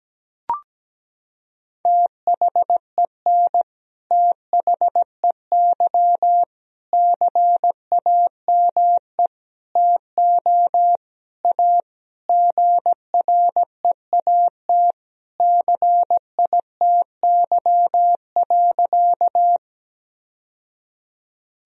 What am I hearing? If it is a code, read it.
Morse code, heard as a keyed tone, about 17 wpm, 700 Hz. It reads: THEN THEY CAME TO A GREAT CITY.
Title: Sentences from Top 1000 Words - Review Part 1 - 17wpm